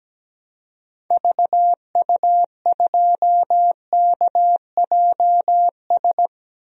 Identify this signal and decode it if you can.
Morse code, heard as a keyed tone, about 17 wpm, 700 Hz. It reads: VU2KJS
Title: DXCC-100 Call Signs - 6 Characters - Repeated 3x - 17wpm